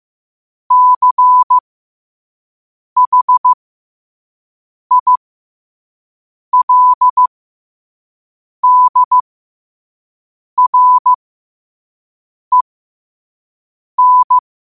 Morse code, heard as a keyed tone, about 15 wpm, spaced as for 5 wpm, 1000 Hz. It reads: CHILDREN
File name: qotdfiles / 5wpm